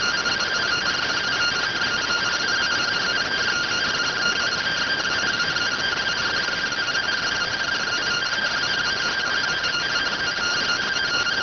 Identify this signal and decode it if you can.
Two-tone teleprinter signal, about 170 Hz shift.